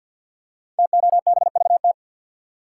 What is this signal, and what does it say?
Morse code, heard as a keyed tone, about 52 wpm, 700 Hz. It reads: TOBVT
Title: Code Groups - 5 Letters - 50wpm